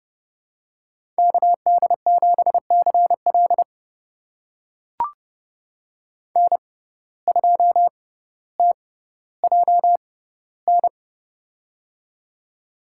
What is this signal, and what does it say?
Morse code, heard as a keyed tone, about 30 wpm, spaced as for 15 wpm, 700 Hz. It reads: KD7CL N2TJN
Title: USA Call Signs - 5 Characters - 6x Character Spacing - 30wpm